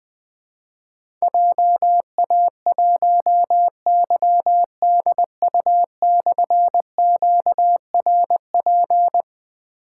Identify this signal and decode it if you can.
Morse code, heard as a keyed tone, about 20 wpm, 700 Hz. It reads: JA1YDU/QRP
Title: Difficult Call Signs 20wpm